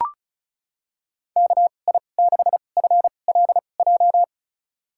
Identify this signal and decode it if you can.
Morse code, heard as a keyed tone, about 36 wpm, spaced as for 25 wpm, 700 Hz. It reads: KI6FLJ
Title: USA Call Signs - 6 Characters - 2x Character Spacing - 35wpm